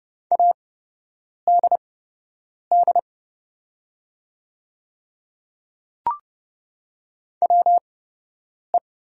Morse code, heard as a keyed tone, about 29 wpm, spaced as for 8 wpm, 690 Hz. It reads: ADD WE
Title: 8x Character Spacing - Top 200 Words - 30wpm